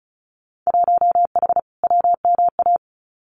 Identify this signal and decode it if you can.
Morse code, heard as a keyed tone, about 34 wpm, 700 Hz. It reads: 1H WMA
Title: Winter Field Day Exchanges Encore 35wpm